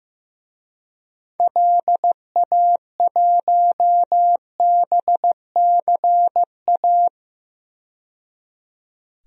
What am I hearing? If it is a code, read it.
Morse code, heard as a keyed tone, about 15 wpm, 700 Hz. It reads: LA1BCA
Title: DXCC-100 Call Signs - 6 Characters - 15wpm